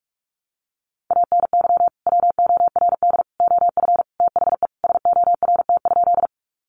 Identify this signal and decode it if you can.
Morse code, heard as a keyed tone, about 44 wpm, 700 Hz. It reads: ANY WORD OF THE SORT?
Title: Sentences from Top 1300 Words 45wpm